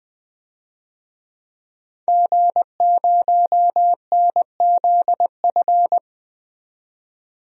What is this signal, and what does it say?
Morse code, heard as a keyed tone, about 20 wpm, 700 Hz. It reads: G0NZF
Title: DXCC-100 Call Signs - 5 Characters - 20wpm